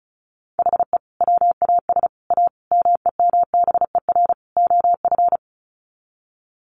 Morse code, heard as a keyed tone, about 34 wpm, 700 Hz. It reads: HE WAS A MEMBER OF
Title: Sets of 5 Words Encore2 35wpm